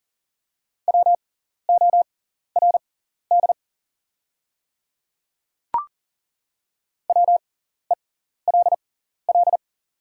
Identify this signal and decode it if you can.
Morse code, heard as a keyed tone, about 40 wpm, spaced as for 14 wpm, 690 Hz. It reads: WORD WELL
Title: 6x Character Spacing - Top 200 Words - 40wpm